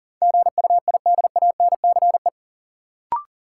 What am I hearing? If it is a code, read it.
Morse code, heard as a keyed tone, about 40 wpm, 700 Hz. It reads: GUIDANCE
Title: Sentences from Top 1600 Words - New Word Drill - 40wpm